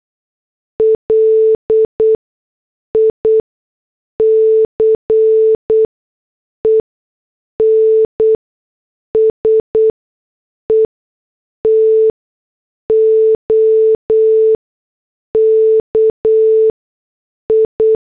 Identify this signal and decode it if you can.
Morse code, heard as a keyed tone, about 8 wpm, spaced as for 6 wpm, 440 Hz. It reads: LICENSE TO KI
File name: morse / headlines